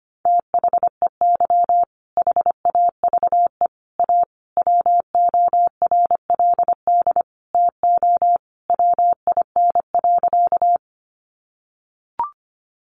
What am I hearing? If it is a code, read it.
Morse code, heard as a keyed tone, about 25 wpm, 700 Hz. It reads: THEY HAVE A WORLD TO WIN.
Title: Sentences from Top 1500 Words 25wpm